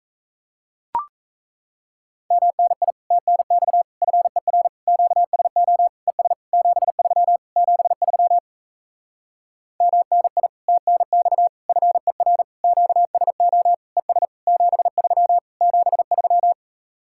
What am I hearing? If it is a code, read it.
Morse code, heard as a keyed tone, about 37 wpm, 700 Hz. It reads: MNI TNX FER QSO ES 73 73 MNI TNX FER QSO ES 73 73